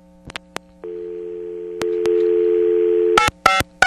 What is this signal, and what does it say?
Dial tone (North America), 350 + 440 Hz.
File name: TurtleCreekEWSD